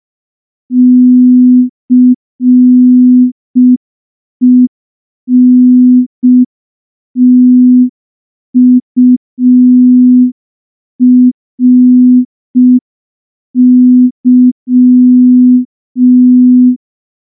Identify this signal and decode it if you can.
Morse code, heard as a keyed tone, about 5 wpm, 250 Hz. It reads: CENTURY
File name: morse / morse-training-05WPM